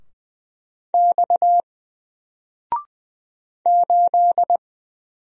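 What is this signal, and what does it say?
Morse code, heard as a keyed tone, about 20 wpm, 700 Hz. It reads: X 8
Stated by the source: Single letter and number Morse drill at 20 wpm, from N4LCD.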